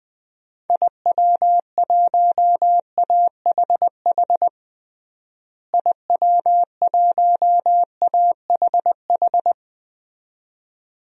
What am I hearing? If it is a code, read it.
Morse code, heard as a keyed tone, about 20 wpm, 700 Hz. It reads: IW1AHH IW1AHH
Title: DXCC-100 Call Signs - 6 Characters - Repeated 2x - 20wpm